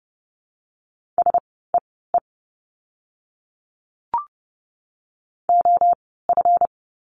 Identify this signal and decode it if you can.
Morse code, heard as a keyed tone, about 30 wpm, spaced as for 17 wpm, 700 Hz. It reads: SEE OF
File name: course / 30:10wpm / 237 Lesson 38f - Letter q - Words 30wpm